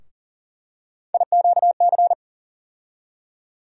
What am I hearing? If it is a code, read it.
Morse code, heard as a keyed tone, about 40 wpm, 700 Hz. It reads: IQC
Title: USA Call Sign Postfixes - 3 Letters 40wpm